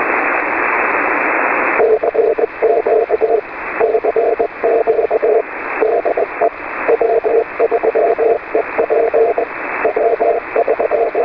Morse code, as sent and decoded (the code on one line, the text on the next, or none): none